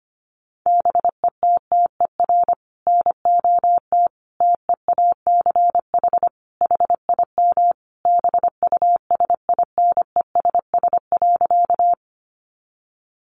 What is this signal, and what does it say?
Morse code, heard as a keyed tone, about 25 wpm, 700 Hz. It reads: BETTER NOT TEACH HIM BUSINESS.
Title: Sentences from Top 600 Words 25wpm